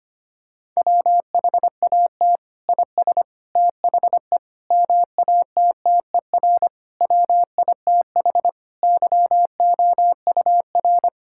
Morse code, heard as a keyed tone, about 25 wpm, 700 Hz. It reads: WHAT IS THE MATTER WITH YOUR
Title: Sentences from Top 1000 Words - Review Part 5 - 25wpm